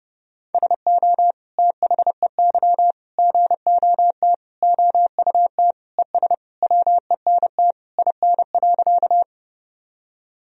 Morse code, heard as a keyed tone, about 30 wpm, 700 Hz. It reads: SO THEY GOT OUT ES WENT IN.
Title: Sentences from Top 300 Words and Abbreviations 30wpm